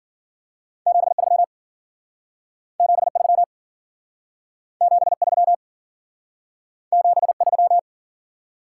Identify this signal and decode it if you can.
Morse code, heard as a keyed tone, about 52 wpm, 700 Hz. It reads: 73 73 73 73